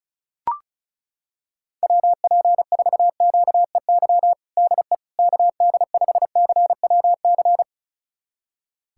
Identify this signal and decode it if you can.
Morse code, heard as a keyed tone, about 36 wpm, 700 Hz. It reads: WP4QEY DE KD5CWC
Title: USA Callsign DE Callsign 35wpm